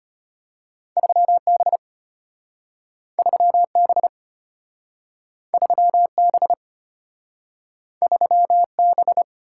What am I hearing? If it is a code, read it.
Morse code, heard as a keyed tone, about 34 wpm, 700 Hz. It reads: 3B 3B 3B 3B